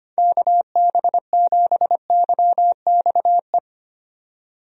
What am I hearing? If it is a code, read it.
Morse code, heard as a keyed tone, about 25 wpm, 700 Hz. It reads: KB7YXE
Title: USA Call Signs - 6 Characters - 25wpm